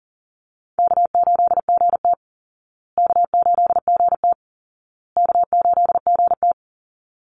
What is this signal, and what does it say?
Morse code, heard as a keyed tone, about 40 wpm, 700 Hz. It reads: K8GT K8GT K8GT
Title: Call Signs Repeated 2x - 40wpm